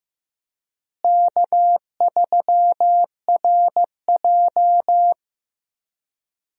Morse code, heard as a keyed tone, about 15 wpm, 700 Hz. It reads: K3RJ
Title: USA Call Signs - 4 Characters - Repeated 3x - 15wpm